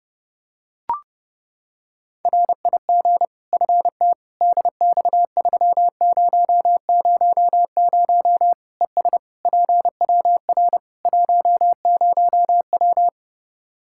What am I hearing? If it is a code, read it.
Morse code, heard as a keyed tone, about 30 wpm, 700 Hz. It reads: RIG FT DX3000 ES PWR 10W
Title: Practice QSO - Exchange 2 - Info - 30wpm